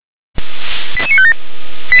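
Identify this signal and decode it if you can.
Five-tone selective call (ZVEI), heard as quick runs of five tones, tones of about 70 ms. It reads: 88957
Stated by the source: ZVEI five-tone selective call received with an SDR.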